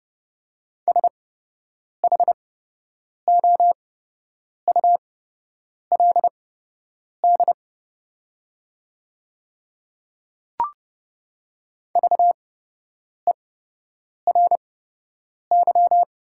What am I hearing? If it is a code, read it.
Morse code, heard as a keyed tone, about 30 wpm, spaced as for 8 wpm, 700 Hz. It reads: SHOULD VERY
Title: 8x Character Spacing - Top 500 Words - 30wpm